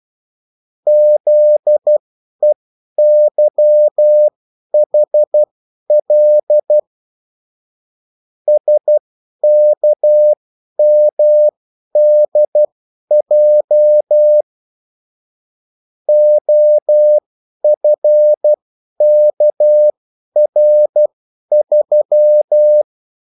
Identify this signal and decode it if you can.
Morse code, heard as a keyed tone, about 12 wpm, spaced as for 10 wpm, 600 Hz. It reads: ZEYHL SKMDJ OFKR3
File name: morse mp3 / revision9-vitesse 10 mots